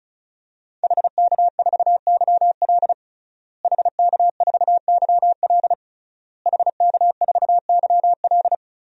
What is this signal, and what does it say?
Morse code, heard as a keyed tone, about 36 wpm, 700 Hz. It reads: HK4YL HK4YL HK4YL
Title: DXCC-100 Call Signs - 5 Characters - Repeated 3x - 35wpm